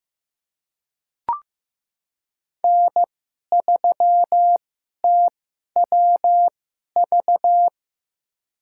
Morse code, heard as a keyed tone, about 15 wpm, spaced as for 11 wpm, 700 Hz. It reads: N3TWV
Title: USA Call Signs - 5 Characters - 2x Character Spacing - 15wpm